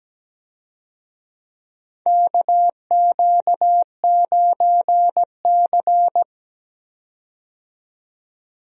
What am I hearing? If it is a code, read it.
Morse code, heard as a keyed tone, about 17 wpm, 700 Hz. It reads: KQ9C
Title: USA Call Signs - 4 Characters - 17wpm